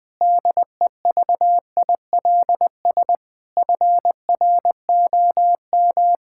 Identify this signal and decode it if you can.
Morse code, heard as a keyed tone, about 20 wpm, 700 Hz. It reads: DEVILS FROM
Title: Pangram Challenge v2 - 20wpm